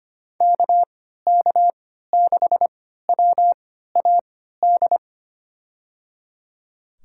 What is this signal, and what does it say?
Morse code, heard as a keyed tone, about 25 wpm, spaced as for 14 wpm, 700 Hz. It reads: KK6WAD
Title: USA Call Signs - 6 Characters - 3x Character Spacing - 25wpm